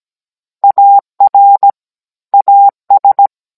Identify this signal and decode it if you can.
Morse code, heard as a keyed tone, about 17 wpm, 800 Hz. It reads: AR AS